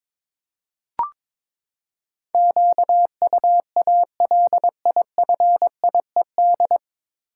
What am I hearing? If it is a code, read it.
Morse code, heard as a keyed tone, about 22 wpm, 700 Hz. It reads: QUALIFIED
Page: Morse Code Ninja practice 22wpm